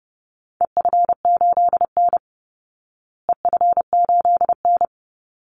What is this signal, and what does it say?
Morse code, heard as a keyed tone, about 30 wpm, 700 Hz. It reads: EF8N EF8N
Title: Call Signs Repeated 3x - 30wpm